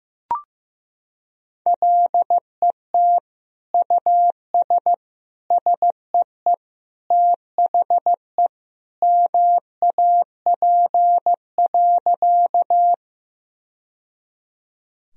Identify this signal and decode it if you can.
Morse code, heard as a keyed tone, about 15 wpm, 700 Hz. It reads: LET US SEE THE MAP.
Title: Sentences from Top 1000 Words - Review Part 2 - 15wpm